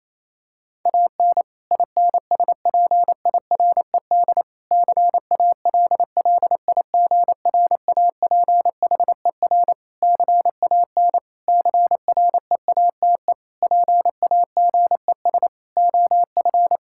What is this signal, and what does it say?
Morse code, heard as a keyed tone, about 28 wpm, 700 Hz. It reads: AN INSPIRED CALLIGRAPHER CAN CREATE PAGES OF
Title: Pangram Challenge v2 - 28wpm